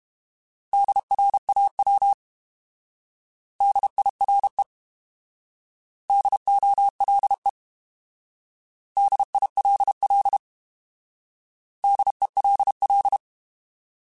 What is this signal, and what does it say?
Morse code, heard as a keyed tone, about 32 wpm, 800 Hz. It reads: DRAW DIRE DOLE DILL DELL